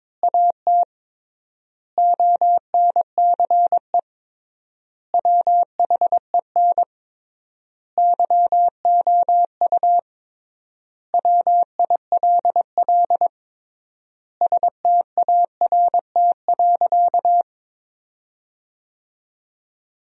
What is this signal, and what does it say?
Morse code, heard as a keyed tone, about 22 wpm, 700 Hz. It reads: AT ONCE WHEN YOU WILL START.
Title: 3x Word-Spacing - Sentences from Top 300 Words - 22wpm